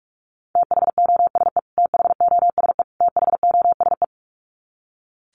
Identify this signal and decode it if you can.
Morse code, heard as a keyed tone, about 44 wpm, 700 Hz. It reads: THOSE THOSE THOSE
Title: Top 300 Words Repeated 45wpm